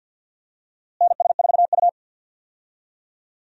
Morse code, heard as a keyed tone, about 49 wpm, 700 Hz. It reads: NS4U
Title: USA Call Signs - 4 Characters - Repeated 3x - 50wpm